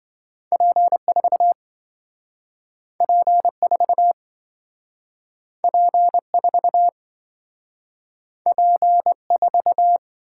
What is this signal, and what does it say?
Morse code, heard as a keyed tone, about 27 wpm, 700 Hz. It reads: P4 P4 P4 P4